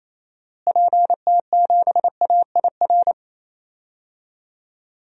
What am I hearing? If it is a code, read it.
Morse code, heard as a keyed tone, about 28 wpm, 700 Hz. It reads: PT7AIR
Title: DXCC-100 Call Signs - 6 Characters - Repeated 2x - 28wpm